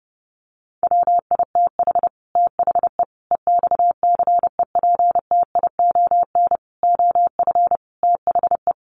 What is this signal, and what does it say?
Morse code, heard as a keyed tone, about 30 wpm, 700 Hz. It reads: WITH THE EXCEPTION OF THE
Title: Sets of 5 Words Encore 30wpm